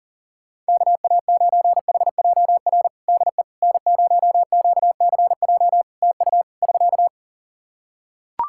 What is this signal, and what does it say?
Morse code, heard as a keyed tone, about 40 wpm, 700 Hz. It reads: KA9HJR DE N0QCJ TU <SK>
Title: Practice QSO - Exchange 4 - Ending - 40wpm